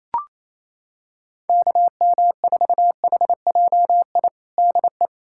Morse code, heard as a keyed tone, about 28 wpm, 700 Hz. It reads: KM4HJI DE
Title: USA Callsign DE Callsign - 6 Characters - 28wpm